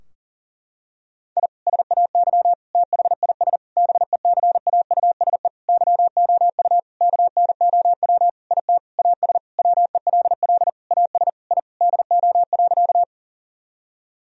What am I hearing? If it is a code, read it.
Morse code, heard as a keyed tone, about 40 wpm, 700 Hz. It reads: I SAY THIS BECAUSE YOU KNOW IT AS WELL AS I DO.
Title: Sentences from Top 1000 Words - Review Part 3 - 40wpm